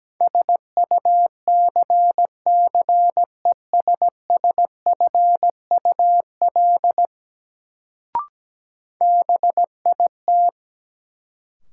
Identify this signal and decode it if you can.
Morse code, heard as a keyed tone, about 17 wpm, 700 Hz. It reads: SUCCESSFUL BIT
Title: Sentences from Top 900 Words - New Word Drill - 17wpm